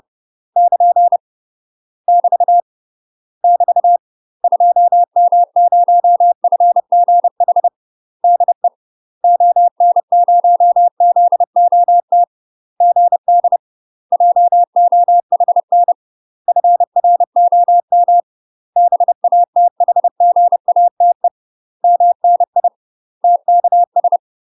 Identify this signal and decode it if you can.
Morse code, heard as a keyed tone, about 30 wpm, 700 Hz. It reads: <KN> <BT> <BT> 2M0FGH DE ON0ZOT GD JOHN FROM BATHGATE MNI TKS
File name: UK MASTER QSOs TRAINING / 30WPM / UK MASTER QSOs 0002